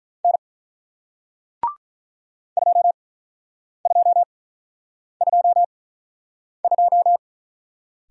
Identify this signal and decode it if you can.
Morse code, heard as a keyed tone, about 48 wpm, 700 Hz. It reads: N 2 2 2 2